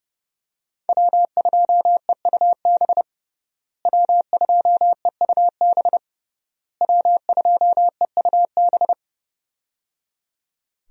Morse code, heard as a keyed tone, about 29 wpm, 700 Hz. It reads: W2EUB W2EUB W2EUB
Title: USA Call Signs - 5 Characters - Repeated 3x - 30wpm